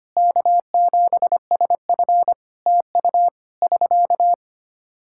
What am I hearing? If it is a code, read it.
Morse code, heard as a keyed tone, about 25 wpm, 700 Hz. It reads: K7SF TU <SK>
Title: Practice QSO - Exchange 4 - Ending - 25wpm